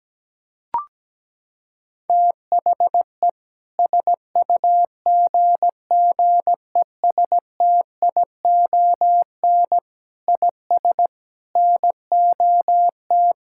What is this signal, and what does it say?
Morse code, heard as a keyed tone, about 17 wpm, 700 Hz. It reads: THE SUGGESTION IS NOT
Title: Sentences from Top 1600 Words 17wpm